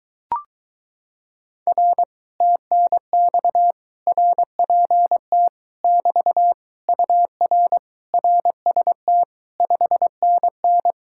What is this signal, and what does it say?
Morse code, heard as a keyed tone, about 22 wpm, 700 Hz. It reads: R TNX RPT <BT> UR RST 5NN